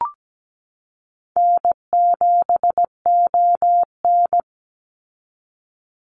Morse code, heard as a keyed tone, about 17 wpm, 700 Hz. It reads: N7ON
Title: Easy Call Signs - 17wpm